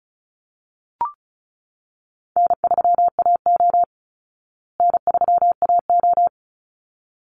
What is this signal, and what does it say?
Morse code, heard as a keyed tone, about 34 wpm, 700 Hz. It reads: N3AO N3AO
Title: Easy Call Signs Repeated 2x - 35wpm